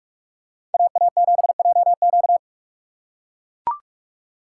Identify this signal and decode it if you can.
Morse code, heard as a keyed tone, about 44 wpm, 700 Hz. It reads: AA7JQ